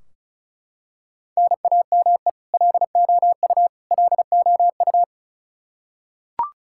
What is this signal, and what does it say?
Morse code, heard as a keyed tone, about 36 wpm, 700 Hz. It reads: NAME LOU LOU
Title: Practice QSO - Exchange 1 - Beginning - 35wpm